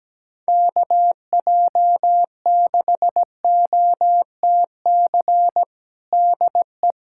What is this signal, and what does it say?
Morse code, heard as a keyed tone, about 17 wpm, 700 Hz. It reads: KJ6OTC DE
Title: USA Callsign DE Callsign 17wpm